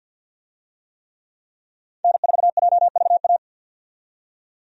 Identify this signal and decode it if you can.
Morse code, heard as a keyed tone, about 52 wpm, 700 Hz. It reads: N4JVA
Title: USA Call Signs - 5 Characters - 50wpm